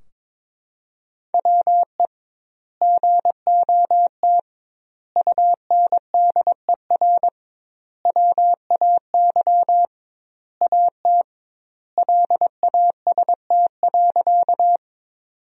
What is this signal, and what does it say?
Morse code, heard as a keyed tone, about 22 wpm, 700 Hz. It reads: WE GOT UNDER WAY AT LAST.
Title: 2x Word-Spacing - Sentences from Top 300 Words - 22wpm